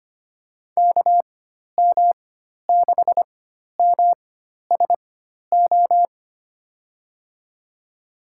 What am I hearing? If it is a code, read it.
Morse code, heard as a keyed tone, about 25 wpm, spaced as for 12 wpm, 700 Hz. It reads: KM6MSO
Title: USA Call Signs - 6 Characters - 4x Character Spacing - 25wpm